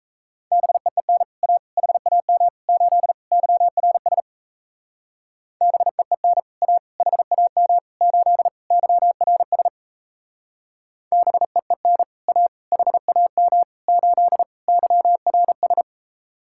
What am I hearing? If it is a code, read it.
Morse code, heard as a keyed tone, about 36 wpm, 700 Hz. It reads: BEEN A HAM 8 YRS BEEN A HAM 8 YRS BEEN A HAM 8 YRS